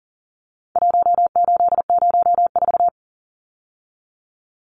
Morse code, heard as a keyed tone, about 40 wpm, 700 Hz. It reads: 1804
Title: Numbers - Contest Style - 4-Digit 40wpm